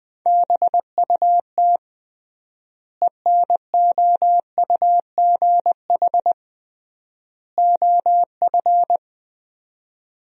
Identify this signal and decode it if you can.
Morse code, heard as a keyed tone, about 20 wpm, 700 Hz. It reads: BUT ENOUGH OF
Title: 3x Word-Spacing - Sentences from Top 400 Words - 20wpm